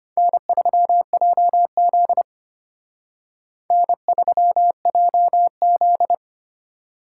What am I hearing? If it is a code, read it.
Morse code, heard as a keyed tone, about 32 wpm, 700 Hz. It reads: N3JZ N3JZ